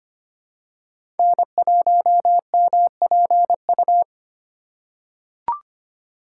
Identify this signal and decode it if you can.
Morse code, heard as a keyed tone, about 25 wpm, 700 Hz. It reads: N1MPU
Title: USA Call Signs - 5 Characters - 2x Character Spacing - 25wpm